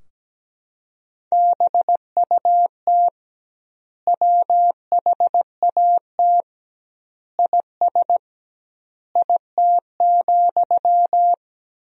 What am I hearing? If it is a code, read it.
Morse code, heard as a keyed tone, about 17 wpm, 700 Hz. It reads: BUT WHAT IS IT,
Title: 2x Word-Spacing - Sentences from Top 300 Words - 17wpm